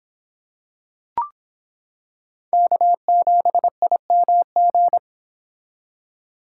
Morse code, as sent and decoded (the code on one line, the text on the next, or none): -.- --... .. -- --.
K7IMG